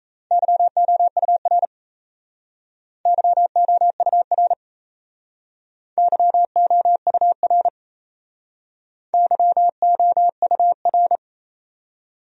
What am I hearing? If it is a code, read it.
Morse code, heard as a keyed tone, about 38 wpm, 700 Hz. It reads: YOUR YOUR YOUR YOUR